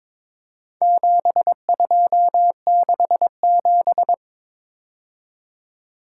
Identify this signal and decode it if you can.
Morse code, heard as a keyed tone, about 21 wpm, 700 Hz. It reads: 7267
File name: practice / 22wpm / Four Letter-Number Combinations - 22wpm